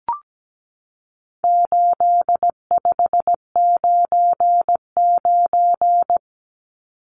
Morse code, heard as a keyed tone, about 17 wpm, 700 Hz. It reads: 8599
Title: Numbers - 4-Digit 17wpm